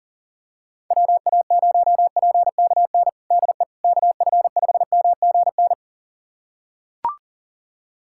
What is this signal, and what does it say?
Morse code, heard as a keyed tone, about 40 wpm, 700 Hz. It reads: WA0PKN DE KF5MGN
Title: USA Callsign DE Callsign - 6 Characters - 40wpm